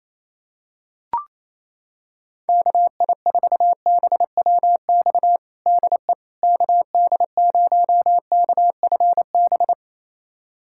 Morse code, heard as a keyed tone, about 28 wpm, 700 Hz. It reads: KI4BWX DE KD0KFB
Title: USA Callsign DE Callsign - 6 Characters - 28wpm